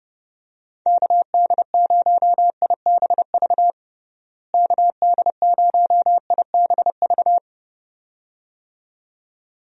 Morse code, heard as a keyed tone, about 30 wpm, 700 Hz. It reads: KD0IBV KD0IBV
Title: USA Call Signs - 6 Characters - Repeated 2x - 30wpm